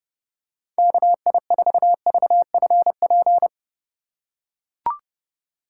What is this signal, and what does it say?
Morse code, heard as a keyed tone, about 30 wpm, 700 Hz. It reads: KI4VFP
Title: USA Call Signs - 2x Character Spacing - 30wpm